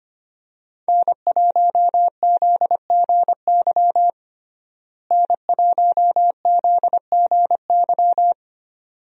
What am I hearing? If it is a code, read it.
Morse code, heard as a keyed tone, about 25 wpm, 700 Hz. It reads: N1ZGY N1ZGY
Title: USA Call Signs - 5 Characters - Repeated 2x - 25wpm